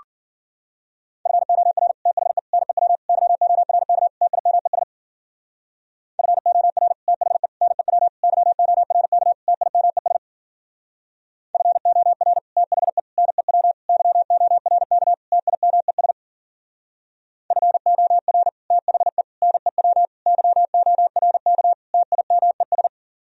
Morse code, heard as a keyed tone, about 48 wpm, 700 Hz. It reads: FOR THE NEW YORK TIMES FOR THE NEW YORK TIMES FOR THE NEW YORK TIMES FOR THE NEW YORK TIMES
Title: Speed-Racing - Sets of 5 Words 40wpm